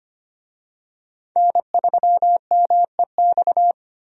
Morse code, heard as a keyed tone, about 25 wpm, 700 Hz. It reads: N3MEX
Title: USA Call Signs - 5 Characters - 25wpm